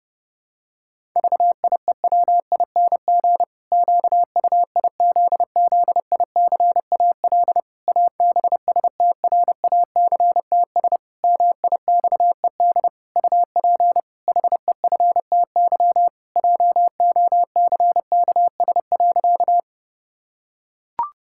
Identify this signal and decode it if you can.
Morse code, heard as a keyed tone, about 30 wpm, 700 Hz. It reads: VIEWING QUIZZICAL ABSTRACTS MIXED UP HEFTY JOCKS.
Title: Pangram Challenge v2 - 30wpm